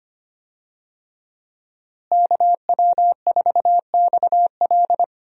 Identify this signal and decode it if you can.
Morse code, heard as a keyed tone, about 25 wpm, 700 Hz. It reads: KW4XL
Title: USA Call Signs - 5 Characters - Repeated 3x - 25wpm